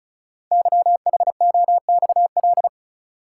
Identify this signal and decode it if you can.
Morse code, heard as a keyed tone, about 36 wpm, 690 Hz. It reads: YHOXL
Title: Code Groups - 5 Letters - 35wpm